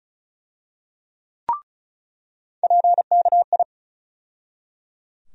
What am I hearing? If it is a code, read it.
Morse code, heard as a keyed tone, about 34 wpm, 700 Hz. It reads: PKI
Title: USA Call Sign Postfixes - 3 Letters 35wpm